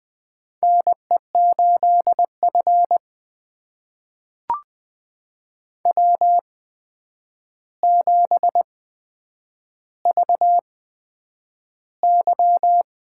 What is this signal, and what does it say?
Morse code, heard as a keyed tone, about 20 wpm, spaced as for 9 wpm, 700 Hz. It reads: NE8F W7VY
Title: USA Call Signs - 4 Characters - 8x Character Spacing - 20wpm